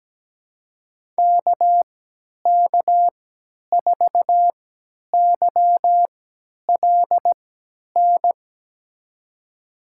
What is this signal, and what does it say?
Morse code, heard as a keyed tone, about 17 wpm, spaced as for 10 wpm, 700 Hz. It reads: KK4YLN
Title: USA Call Signs - 6 Characters - 3x Character Spacing - 17wpm